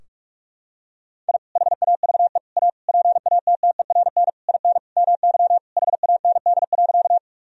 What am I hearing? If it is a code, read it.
Morse code, heard as a keyed tone, about 41 wpm, 700 Hz. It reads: I HAVE A PATTERN IN MY HAND.
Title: Sentences from Top 1000 Words - Review Part 2 - 45wpm